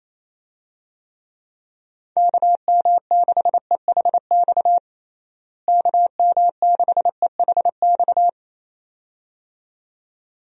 Morse code, heard as a keyed tone, about 28 wpm, 700 Hz. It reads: KM6EHX KM6EHX
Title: USA Call Signs - 6 Characters - Repeated 2x - 28wpm